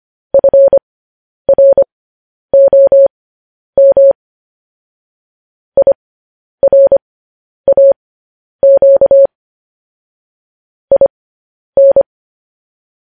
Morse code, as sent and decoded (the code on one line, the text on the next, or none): ..-. .-. --- -- / .. .-. .- --.- / .. -.
FROM IRAQ IN